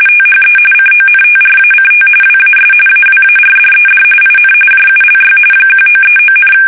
RTTY, about 45.45 baud. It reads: THE CONTRACT WAS AWARDED IN JULY 1939,
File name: fsk850 / battleshipnj